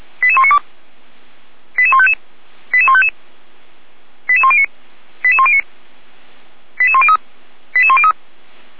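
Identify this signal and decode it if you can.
Five-tone selective call (ZVEI), heard as quick runs of five tones, tones of about 70 ms. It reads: 88192 88155 88155 88109 88109 88103 88103
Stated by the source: ZVEI five-tone selective call received on SDR.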